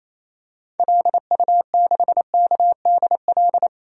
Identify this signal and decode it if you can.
Morse code, heard as a keyed tone, about 28 wpm, 700 Hz. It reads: LU6KDL
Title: DXCC-100 Call Signs - 6 Characters - Repeated 3x - 28wpm